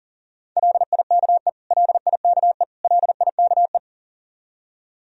Morse code, heard as a keyed tone, about 40 wpm, 700 Hz. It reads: LIKE LIKE LIKE
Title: Top 100 Words Repeated 40wpm